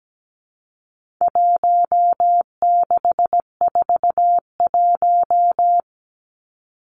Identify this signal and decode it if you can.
Morse code, heard as a keyed tone, about 17 wpm, 700 Hz. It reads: 1641
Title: Numbers - Contest Style - 4-Digit 17wpm